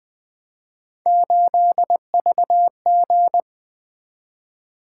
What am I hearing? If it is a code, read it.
Morse code, heard as a keyed tone, about 20 wpm, 700 Hz. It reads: 8VG
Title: Three Letter-Number Combinations - 20wpm